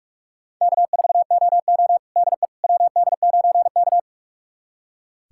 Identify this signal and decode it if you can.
Morse code, heard as a keyed tone, about 45 wpm, 700 Hz. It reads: K4OO DE WD9K
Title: USA Callsign DE Callsign - 4 Characters - 45wpm